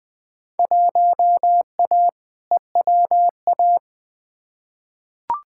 Morse code, heard as a keyed tone, about 20 wpm, 700 Hz. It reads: 1A EWA
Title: Field Day Exchanges Encore - 20wpm